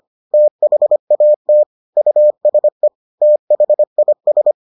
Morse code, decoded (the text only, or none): THAT USE THIS